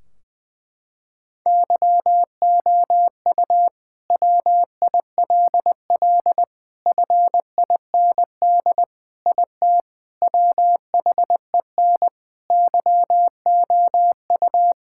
Morse code, heard as a keyed tone, about 20 wpm, 700 Hz. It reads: YOU WILL FIND IT WHEN YOU
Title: Sentences from Top 1600 Words 20wpm